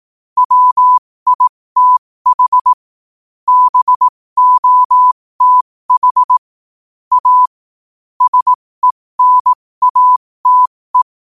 Morse code, heard as a keyed tone, about 18 wpm, spaced as for 15 wpm, 1000 Hz. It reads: WITH BOTH A SENATE
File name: qotdfiles / 15wpm